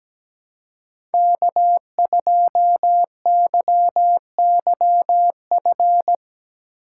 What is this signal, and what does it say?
Morse code, heard as a keyed tone, about 17 wpm, 700 Hz. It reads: K2YYF
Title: USA Call Signs - 4x Character Spacing - 17wpm